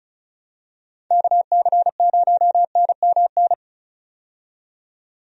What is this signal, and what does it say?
Morse code, heard as a keyed tone, about 34 wpm, 700 Hz. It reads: KC0NMN
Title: USA Call Signs - 6 Characters - 35wpm